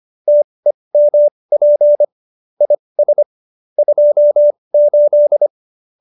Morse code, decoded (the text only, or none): TEMP IS 28